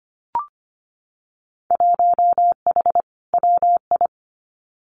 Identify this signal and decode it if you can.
Morse code, heard as a keyed tone, about 25 wpm, 700 Hz. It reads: 1H WI
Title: Winter Field Day Exchanges 25wpm